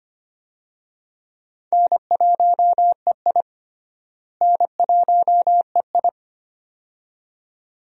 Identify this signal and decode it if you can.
Morse code, heard as a keyed tone, about 25 wpm, 700 Hz. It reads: N1EI N1EI
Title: USA Call Signs - 4 Characters - Repeated 2x - 25wpm